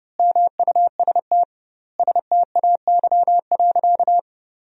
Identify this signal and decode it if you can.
Morse code, heard as a keyed tone, about 30 wpm, 700 Hz. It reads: MUST STAY.
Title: 2x Word-Spacing - Sentences from Top 500 Words - 30wpm